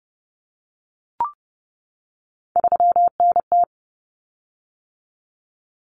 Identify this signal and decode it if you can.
Morse code, heard as a keyed tone, about 30 wpm, 700 Hz. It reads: 3NT
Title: Common Cut Numbers - 3-Digit 30wpm